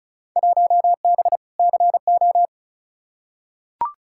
Morse code, heard as a keyed tone, about 36 wpm, 700 Hz. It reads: 1B CO
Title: Field Day Exchanges - 35wpm